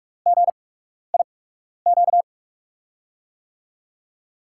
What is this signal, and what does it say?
Morse code, heard as a keyed tone, about 44 wpm, spaced as for 12 wpm, 700 Hz. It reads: GIQ
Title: USA Call Signs - 6 Characters - 8x Character Spacing - 45wpm